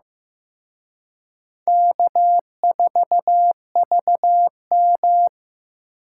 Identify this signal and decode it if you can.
Morse code, heard as a keyed tone, about 18 wpm, 700 Hz. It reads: K4VM
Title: Speed Racing - USA 1x2 Call Signs 15wpm